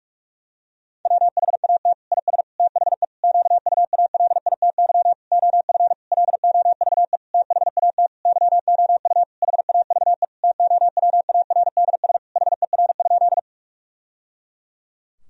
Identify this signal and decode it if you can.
Morse code, heard as a keyed tone, about 46 wpm, 700 Hz. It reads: WHAT IS THE QUALITY OF LOVE THAT YOU HAVE TOWARDS HER?
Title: Sentences from Top 1000 Words - Review Part 4 - 45wpm